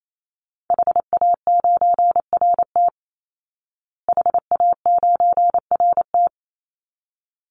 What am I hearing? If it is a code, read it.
Morse code, heard as a keyed tone, about 28 wpm, 700 Hz. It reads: HA9RT HA9RT
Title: Call Signs Repeated 3x - 28wpm